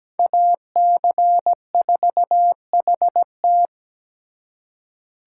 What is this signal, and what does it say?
Morse code, heard as a keyed tone, about 20 wpm, 700 Hz. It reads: AC4HT
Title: Speed Racing - USA Call Signs 17wpm